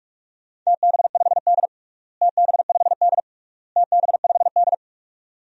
Morse code, heard as a keyed tone, about 46 wpm, 700 Hz. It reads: TB5D TB5D TB5D
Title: DXCC-100 Call Signs - 4 Characters - Repeated 3x - 45wpm